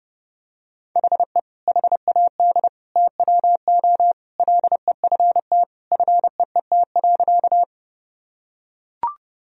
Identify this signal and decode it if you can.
Morse code, heard as a keyed tone, about 30 wpm, 700 Hz. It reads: HE HAD TWO LEFT FEET.
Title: Sentences from Top 1000 Words - Review Part 1 - 30wpm